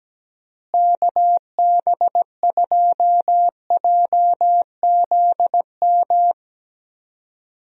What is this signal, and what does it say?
Morse code, heard as a keyed tone, about 17 wpm, 700 Hz. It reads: KB2JZM
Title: USA Call Signs - 6 Characters - Repeated 3x - 17wpm